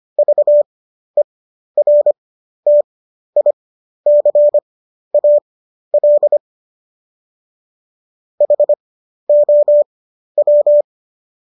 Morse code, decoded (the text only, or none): VERTICAL HOW